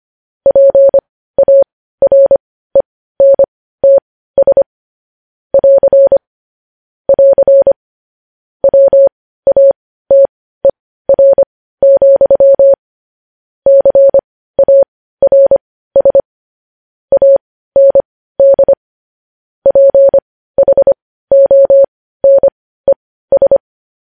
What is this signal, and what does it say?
Morse code, heard as a keyed tone, about 25 wpm, spaced as for 15 wpm, 550 Hz. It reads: PARENTS <AR> <AR> WATER, CARS AND PHONES
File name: podcast / News headlines-1x-25x15